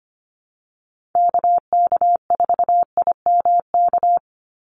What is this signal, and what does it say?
Morse code, heard as a keyed tone, about 25 wpm, 700 Hz. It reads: KK4IMK
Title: USA Call Signs 25wpm